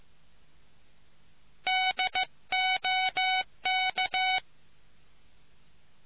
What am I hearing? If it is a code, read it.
Morse code, heard as a keyed tone, about 15 wpm, 750 Hz. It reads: DOK